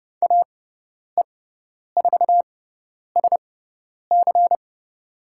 Morse code, heard as a keyed tone, about 30 wpm, spaced as for 10 wpm, 700 Hz. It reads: AE4SC
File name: Lesson 17g - Letter u - Call Signs 30wpm